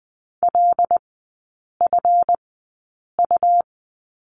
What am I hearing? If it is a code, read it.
Morse code, heard as a keyed tone, about 20 wpm, spaced as for 8 wpm, 700 Hz. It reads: LFU
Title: 138 Lesson 23e - Letter f - Three Characters 20wpm